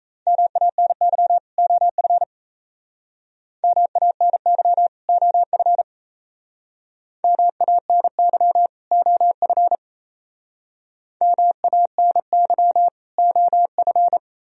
Morse code, heard as a keyed tone, about 36 wpm, 700 Hz. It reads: MANY OF MANY OF MANY OF MANY OF